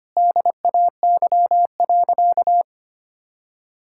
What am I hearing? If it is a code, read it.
Morse code, heard as a keyed tone, about 25 wpm, 700 Hz. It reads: DAY.
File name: practice / 25wpm / Sentences with 3-Word Phrases - 25wpm